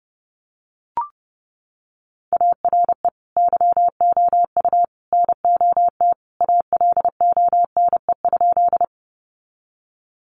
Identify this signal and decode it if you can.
Morse code, heard as a keyed tone, about 30 wpm, 700 Hz. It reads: ARE YOU NOT ALONE?
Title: Sentences from Top 1000 Words 30wpm